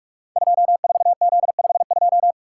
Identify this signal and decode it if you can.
Morse code, heard as a keyed tone, about 44 wpm, 700 Hz. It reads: 24Z52
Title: Code Groups - 5 Letter:Numbers - 45wpm